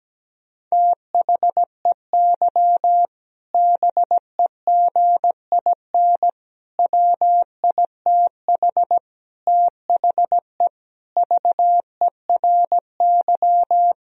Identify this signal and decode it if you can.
Morse code, heard as a keyed tone, about 17 wpm, 700 Hz. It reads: THEY BEGIN WITH THE VERY
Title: Sentences from Top 1000 Words - Review Part 3 - 17wpm